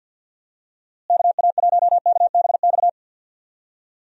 Morse code, heard as a keyed tone, about 52 wpm, 700 Hz. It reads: KA1KBX
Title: USA Call Signs - 6 Characters - Repeated 2x - 50wpm